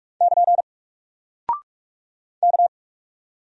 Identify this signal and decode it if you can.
Morse code, heard as a keyed tone, about 44 wpm, 700 Hz. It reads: <KN> K
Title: Single Character (Letter:Number:Punctuation:Prosign) Alt 45wpm